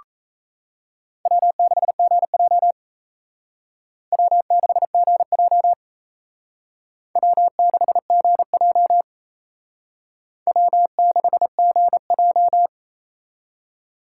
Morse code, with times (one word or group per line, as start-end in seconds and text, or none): 1.3-2.7 W6GJ
4.1-5.7 W6GJ
7.2-9.0 W6GJ
10.5-12.7 W6GJ